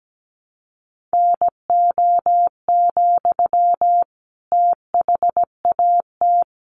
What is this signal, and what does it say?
Morse code, heard as a keyed tone, about 17 wpm, 700 Hz. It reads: NO, THAT
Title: Sentences from Top 600 Words 17wpm